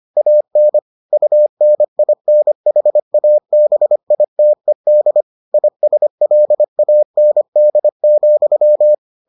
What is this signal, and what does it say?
Morse code, heard as a keyed tone, about 25 wpm, 600 Hz. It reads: AN UNINHABITED ISLAND,